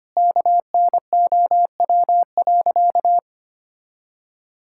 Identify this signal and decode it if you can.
Morse code, heard as a keyed tone, about 25 wpm, 700 Hz. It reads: KNOW.
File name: practice / 25wpm / Sentences with 3-Word Phrases - 25wpm